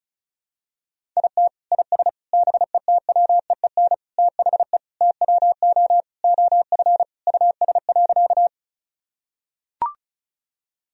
Morse code, heard as a keyed tone, about 36 wpm, 700 Hz. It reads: IT IS BETWEEN THE TWO OF US.
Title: Sentences from Top 1000 Words - Review Part 1 - 35wpm